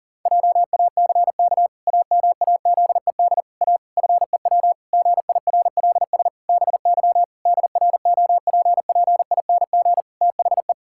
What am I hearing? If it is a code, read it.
Morse code, heard as a keyed tone, about 40 wpm, 700 Hz. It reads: JACK AMAZED A FEW GIRLS BY DROPPING THE
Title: Pangram Challenge v2 - 40wpm